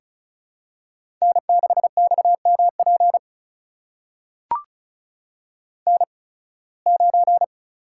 Morse code, heard as a keyed tone, about 36 wpm, spaced as for 23 wpm, 700 Hz. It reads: N6XMP N9
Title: USA Call Signs - 5 Characters - 8x Character Spacing - 35wpm